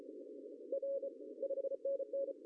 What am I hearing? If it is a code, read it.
Morse code, heard as a keyed tone, about 33 wpm, 550 Hz. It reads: R 5NN
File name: CW / cqww-cw-k5tr